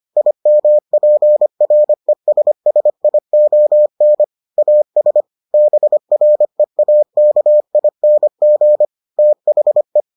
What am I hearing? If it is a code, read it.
Morse code, heard as a keyed tone, about 25 wpm, 600 Hz. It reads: IMPRESSION AS BREAKING THE